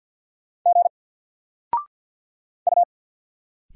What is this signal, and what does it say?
Morse code, heard as a keyed tone, about 48 wpm, 700 Hz. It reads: G U
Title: Single Letter-Number Alt 50wpm